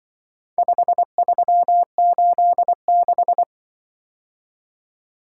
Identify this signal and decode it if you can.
Morse code, heard as a keyed tone, about 24 wpm, 700 Hz. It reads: 5386
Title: Speed-Racing - Four Letter-Number Combinations - 20wpm